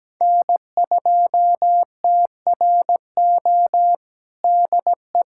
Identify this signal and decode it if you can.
Morse code, heard as a keyed tone, about 17 wpm, 700 Hz. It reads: N2TRO DE
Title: USA Callsign DE Callsign - 5 Characters - 17wpm